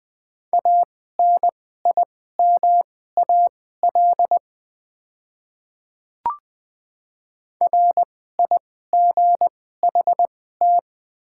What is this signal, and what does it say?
Morse code, heard as a keyed tone, about 20 wpm, spaced as for 14 wpm, 700 Hz. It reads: ANIMAL RIGHT
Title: 2x Character Spacing - Top 200 Words - 20wpm